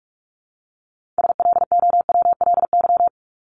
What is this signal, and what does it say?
Morse code, heard as a keyed tone, about 44 wpm, 700 Hz. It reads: SLOWLY
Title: Top 1001-1250 Words 45wpm